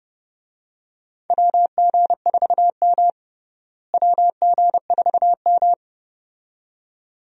Morse code, heard as a keyed tone, about 30 wpm, 700 Hz. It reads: WG4M WG4M
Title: USA Call Signs - 4 Characters - Repeated 2x - 30wpm